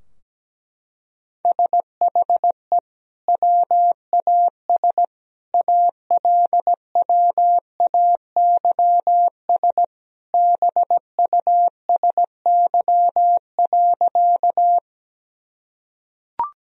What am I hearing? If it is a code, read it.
Morse code, heard as a keyed tone, about 17 wpm, 700 Hz. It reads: SHE WAS ALWAYS BUSY.
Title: Sentences from Top 1000 Words - Review Part 3 - 17wpm